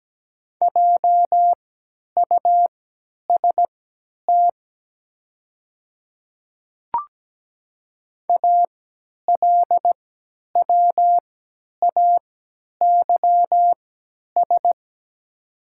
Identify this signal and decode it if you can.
Morse code, heard as a keyed tone, about 17 wpm, spaced as for 10 wpm, 700 Hz. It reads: JUST ALWAYS
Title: 3x Character Spacing - Top 300 Words - 17wpm